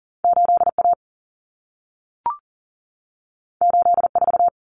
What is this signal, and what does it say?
Morse code, heard as a keyed tone, about 40 wpm, 700 Hz. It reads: 8A 84
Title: Common Cut Numbers - 2-Digit 40wpm